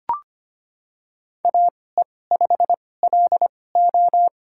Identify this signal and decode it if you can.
Morse code, heard as a keyed tone, about 25 wpm, spaced as for 18 wpm, 700 Hz. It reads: AE5LO
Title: USA Call Signs - 5 Characters - 2x Character Spacing - 25wpm